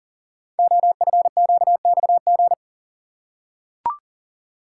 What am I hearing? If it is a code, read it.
Morse code, heard as a keyed tone, about 40 wpm, 700 Hz. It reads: OFQXG
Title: Code Groups - 5 Letters - 40wpm